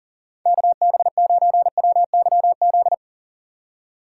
Morse code, heard as a keyed tone, about 38 wpm, 700 Hz. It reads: KB9WYZ
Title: USA Call Signs - 6 Characters - 6x Character Spacing - 40wpm